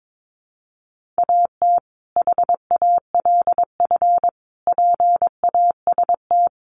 Morse code, heard as a keyed tone, about 22 wpm, 700 Hz. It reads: AT HALF PAST
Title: Sentences from Top 600 Words 22wpm